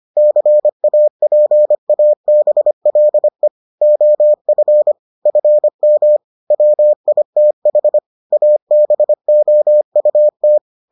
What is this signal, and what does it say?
Morse code, heard as a keyed tone, about 25 wpm, 600 Hz. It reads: CAPABLE OF FM WITH ABOUT